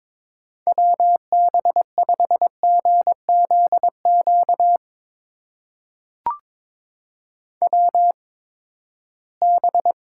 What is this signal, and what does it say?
Morse code, heard as a keyed tone, about 21 wpm, spaced as for 15 wpm, 700 Hz. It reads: WB5GZQ WB
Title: USA Call Signs - 6 Characters - 8x Character Spacing - 22wpm